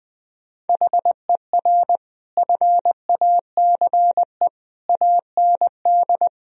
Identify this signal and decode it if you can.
Morse code, heard as a keyed tone, about 21 wpm, 700 Hz. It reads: HER FACE AND
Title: Sentences from Top 1000 Words - Review Part 2 - 20wpm